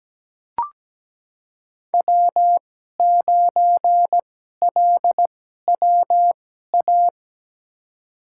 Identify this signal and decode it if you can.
Morse code, heard as a keyed tone, about 17 wpm, spaced as for 12 wpm, 700 Hz. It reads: W9LWA
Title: USA Call Signs - 5 Characters - 2x Character Spacing - 17wpm